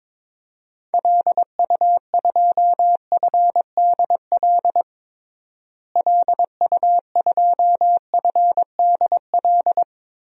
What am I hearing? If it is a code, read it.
Morse code, heard as a keyed tone, about 22 wpm, 700 Hz. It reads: LU2FDL LU2FDL
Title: DXCC-100 Call Signs - 6 Characters - Repeated 3x - 22wpm